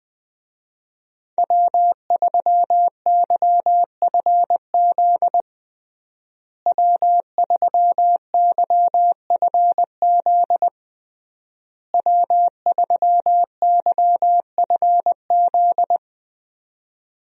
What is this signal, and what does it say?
Morse code, heard as a keyed tone, about 20 wpm, 700 Hz. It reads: W3YFZ W3YFZ W3YFZ
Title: USA Call Signs - 5 Characters - Repeated 3x - 20wpm